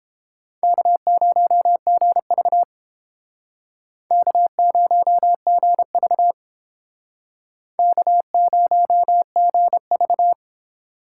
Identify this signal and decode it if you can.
Morse code, heard as a keyed tone, about 30 wpm, 700 Hz. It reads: K0GV K0GV K0GV